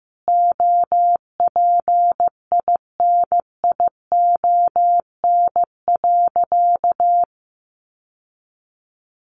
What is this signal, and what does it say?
Morse code, heard as a keyed tone, about 15 wpm, 700 Hz. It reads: OPINION.
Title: Sentences from Top 1200 Words 15wpm